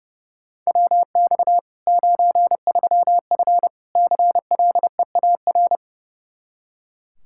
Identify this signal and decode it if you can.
Morse code, heard as a keyed tone, about 30 wpm, 700 Hz. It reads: WX 93F CLEAR
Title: Practice QSO - Exchange 2 - Info - 30wpm